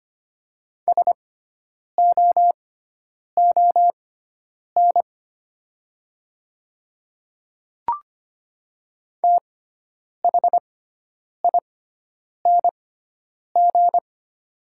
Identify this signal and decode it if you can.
Morse code, heard as a keyed tone, about 25 wpm, spaced as for 9 wpm, 700 Hz. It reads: SOON THING
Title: 6x Character Spacing - Top 500 Words - 25wpm